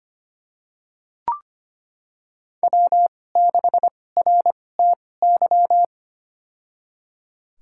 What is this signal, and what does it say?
Morse code, heard as a keyed tone, about 25 wpm, spaced as for 18 wpm, 700 Hz. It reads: W6RTY
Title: USA Call Signs - 2x Character Spacing - 25wpm